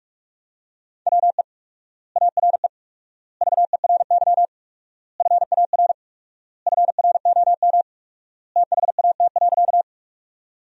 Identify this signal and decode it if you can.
Morse code, heard as a keyed tone, about 42 wpm, 700 Hz. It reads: WE ARE VERY FAR FROM THAT.
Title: 4x Word-Spacing - Sentences from Top 300 Words - 45wpm